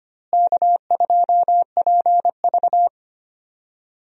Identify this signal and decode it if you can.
Morse code, heard as a keyed tone, about 25 wpm, 700 Hz. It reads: K2PV
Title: USA Call Signs - 4 Characters - Repeated 3x - 25wpm